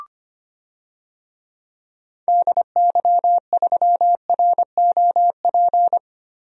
Morse code, heard as a keyed tone, about 25 wpm, 700 Hz. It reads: DY3ROP
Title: DXCC-100 Call Signs - 6 Characters - Repeated 2x - 25wpm